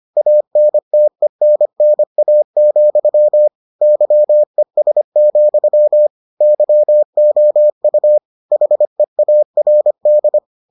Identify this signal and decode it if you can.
Morse code, heard as a keyed tone, about 25 wpm, 600 Hz. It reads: ANTENNA, YES, YOU HEARD